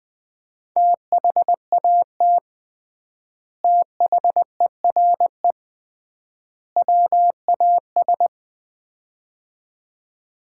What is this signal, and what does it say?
Morse code, heard as a keyed tone, about 21 wpm, 700 Hz. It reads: THAT THERE WAS
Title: 3x Word-Spacing - Sets of 3 Words - 20wpm